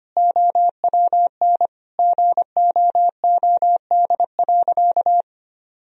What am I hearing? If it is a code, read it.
Morse code, heard as a keyed tone, about 25 wpm, 700 Hz. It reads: OWN GOOD.
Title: Sentences from Top 1000 Words - Review Part 4 - 25wpm